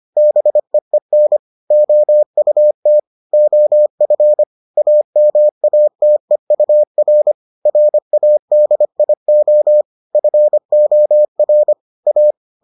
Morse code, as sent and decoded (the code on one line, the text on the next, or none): -... . -. / --- ..- - / --- ..-. / .- -- .- - . ..- .-. / .-. .- -.. .. --- / ..-. --- .-. / .-
BEEN OUT OF AMATEUR RADIO FOR A